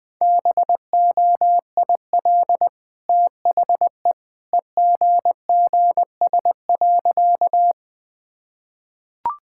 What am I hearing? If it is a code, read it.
Morse code, heard as a keyed tone, about 20 wpm, 700 Hz. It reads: BOIL THE EGGS.
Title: Easy Sentences with Hints - 20wpm